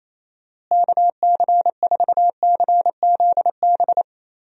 Morse code, heard as a keyed tone, about 28 wpm, 700 Hz. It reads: KC4CZB
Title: USA Call Signs - 6 Characters - Repeated 2x - 28wpm